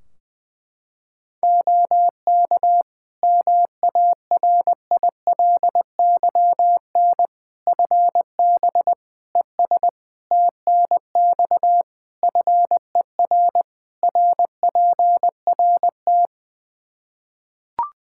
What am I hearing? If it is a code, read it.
Morse code, heard as a keyed tone, about 20 wpm, 700 Hz. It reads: OK MARILYN FB ES TNX FER RPRT
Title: Practice QSO - Exchange 2 - Info - 20wpm